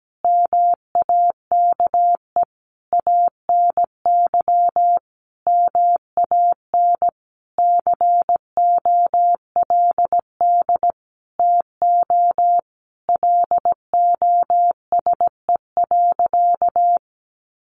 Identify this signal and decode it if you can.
Morse code, heard as a keyed tone, about 17 wpm, 700 Hz. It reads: MAKE ANY MAN COLD TO LOSE.
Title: Sentences from Top 1100 Words 17wpm